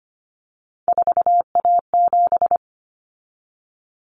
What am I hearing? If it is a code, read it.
Morse code, heard as a keyed tone, about 25 wpm, 700 Hz. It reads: 4A7
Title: Common Cut Numbers - 3-Digit 25wpm